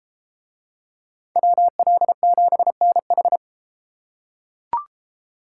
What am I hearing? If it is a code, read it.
Morse code, heard as a keyed tone, about 31 wpm, 700 Hz. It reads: WL7NH